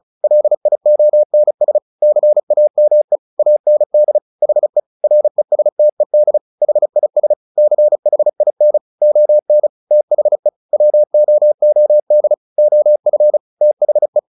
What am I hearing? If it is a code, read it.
Morse code, heard as a keyed tone, about 35 wpm, 600 Hz. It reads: LIONS CAME AND HE RESTED HIS CHIN ON THE WOOD OF THE
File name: the old man and the sea 2 35wpm